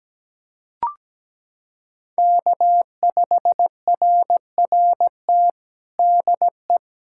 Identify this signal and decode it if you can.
Morse code, heard as a keyed tone, about 17 wpm, 700 Hz. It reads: K5RRT DE
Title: USA Callsign DE Callsign 17wpm